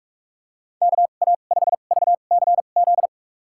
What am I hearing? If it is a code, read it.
Morse code, heard as a keyed tone, about 41 wpm, spaced as for 31 wpm, 700 Hz. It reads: KA5VCZ
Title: USA Call Signs - 6 Characters - 2x Character Spacing - 45wpm